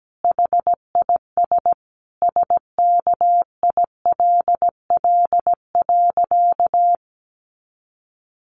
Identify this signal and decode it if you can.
Morse code, heard as a keyed tone, about 17 wpm, 700 Hz. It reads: HIS SKILL.
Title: Sentences from Top 1200 Words 17wpm